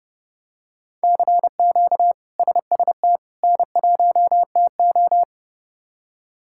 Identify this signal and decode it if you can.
Morse code, heard as a keyed tone, about 30 wpm, 700 Hz. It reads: CQ SST N1TO
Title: K1USN SST QSO Practice - 30wpm